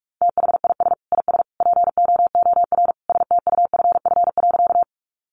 Morse code, heard as a keyed tone, about 44 wpm, 700 Hz. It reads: THIS IS POOR STUFF.
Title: Sentences from Top 1400 Words 45wpm